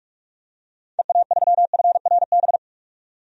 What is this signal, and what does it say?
Morse code, heard as a keyed tone, about 46 wpm, 700 Hz. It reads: EA3FRB
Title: DXCC-100 Call Signs - 6 Characters - 45wpm